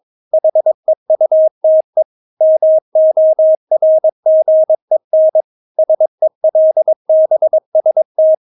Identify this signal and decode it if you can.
Morse code, heard as a keyed tone, about 22 wpm, 630 Hz. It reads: HEUTE MORGEN SELBST
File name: Witz3 22wpm